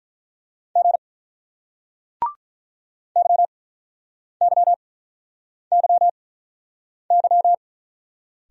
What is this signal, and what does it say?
Morse code, heard as a keyed tone, about 45 wpm, 700 Hz. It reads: G Y Y Y Y